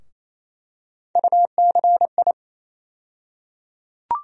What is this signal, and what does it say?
Morse code, heard as a keyed tone, about 28 wpm, 700 Hz. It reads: UCI
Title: USA Call Sign Postfixes - 3 Letters 28wpm